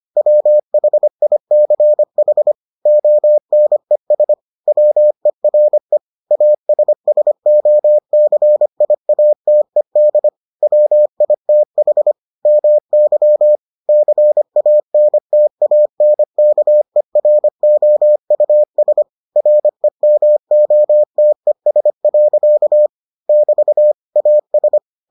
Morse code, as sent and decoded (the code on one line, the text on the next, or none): .-- .... .. -.-. .... / --- -. ... / .-- . .-. / .- ... ... --- -.-. .. .- - . -.. / .-- .. - .... / -- -.-- / -.-. .- -. - .- -. -.- . .-. --- ..- ... / .-. -- --- - . ... .-.-.- / -...- / .- ...
WHICH ONES WERE ASSOCIATED WITH MY CANTANKEROUS REMOTES. <BT> AS